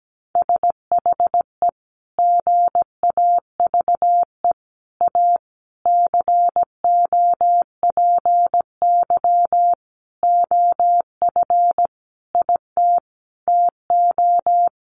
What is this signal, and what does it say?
Morse code, heard as a keyed tone, about 17 wpm, 700 Hz. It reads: SHE GAVE A COPY OF IT TO
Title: Sentences from Top 1400 Words 17wpm